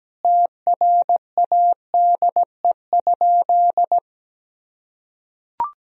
Morse code, heard as a keyed tone, about 17 wpm, 700 Hz. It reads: TRADE?
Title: Sentences from Top 1000 Words - Review Part 4 - 17wpm